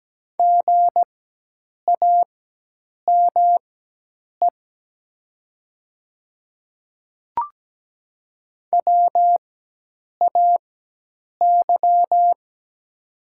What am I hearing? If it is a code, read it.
Morse code, heard as a keyed tone, about 17 wpm, spaced as for 8 wpm, 700 Hz. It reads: GAME WAY
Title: 4x Character Spacing - Top 500 Words - 17wpm